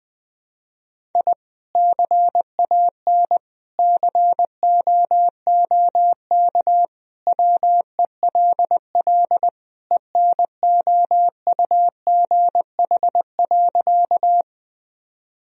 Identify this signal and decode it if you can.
Morse code, heard as a keyed tone, about 20 wpm, 700 Hz. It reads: I CAN COOK WELL ENOUGH.
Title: Sentences from Top 1000 Words - Review Part 4 - 20wpm